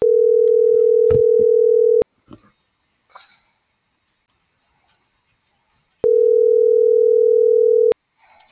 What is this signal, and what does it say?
Ringback tone (North America), 440 + 480 Hz.
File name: CENTRAL TERRESTRE / out els..laz.elo.mo.24